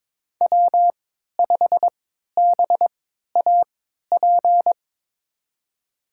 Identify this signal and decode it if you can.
Morse code, heard as a keyed tone, about 22 wpm, spaced as for 12 wpm, 700 Hz. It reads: W5BAP
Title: USA Call Signs - 5 Characters - 3x Character Spacing - 22wpm